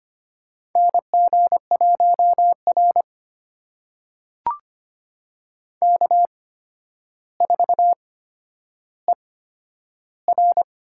Morse code, heard as a keyed tone, about 25 wpm, spaced as for 11 wpm, 700 Hz. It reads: NG1R K4ER
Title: USA Call Signs - 8x Character Spacing - 25wpm